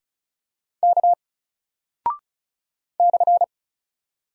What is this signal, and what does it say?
Morse code, heard as a keyed tone, about 36 wpm, 700 Hz. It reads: K /
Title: Single Character (Letter:Number:Punctuation:Prosign) 35wpm